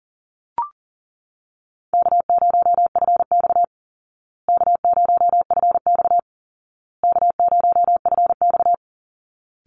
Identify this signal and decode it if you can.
Morse code, heard as a keyed tone, about 40 wpm, 700 Hz. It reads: K0FX K0FX K0FX
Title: Easy Call Signs Repeated 3x - 40wpm